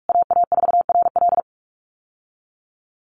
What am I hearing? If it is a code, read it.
Morse code, heard as a keyed tone, about 44 wpm, 700 Hz. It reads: AA4RL
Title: USA Call Signs Repeated 2x 45wpm